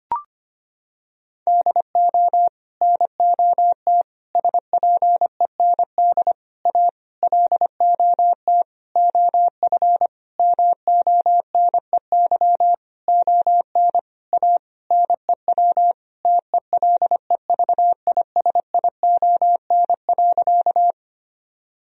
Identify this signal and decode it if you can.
Morse code, heard as a keyed tone, about 25 wpm, 700 Hz. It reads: DO NOT SPEND A LOT OF MONEY ON A NEW TELEVISION.
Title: Sentences from Top 1000 Words - Review Part 4 - 25wpm